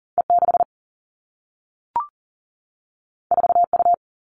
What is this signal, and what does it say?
Morse code, heard as a keyed tone, about 42 wpm, 700 Hz. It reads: E 6 4 U
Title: Two Challenging Letter-Number Combinations 40wpm